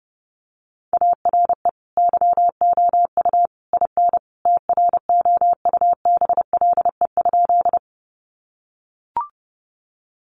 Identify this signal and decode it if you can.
Morse code, heard as a keyed tone, about 30 wpm, 700 Hz. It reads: ARE YOU IN TROUBLE?
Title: Sentences from Top 1200 Words 30wpm